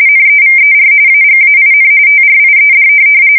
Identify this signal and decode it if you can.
RTTY, about 45.45 baud. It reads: NEVER AGAIN TRIED
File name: fsk170 / USS-New-Jersey